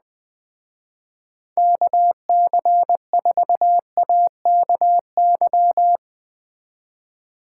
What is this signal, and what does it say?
Morse code, heard as a keyed tone, about 20 wpm, 700 Hz. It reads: KC4AKY
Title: Speed Racing - USA 2x3 Call Signs 17wpm